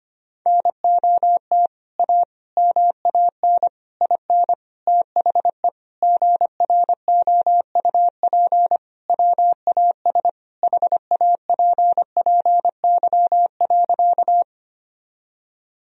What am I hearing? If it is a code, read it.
Morse code, heard as a keyed tone, about 25 wpm, 700 Hz. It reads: NOT A MAN IN THE GROUP WAS HAPPY.
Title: Sentences from Top 1000 Words - Review Part 1 - 25wpm